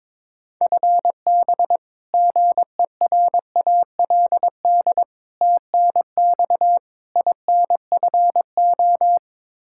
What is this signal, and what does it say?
Morse code, heard as a keyed tone, about 22 wpm, 700 Hz. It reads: FB GERALD TNX INFO
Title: Practice QSO - Exchange 3 - More Info - 22wpm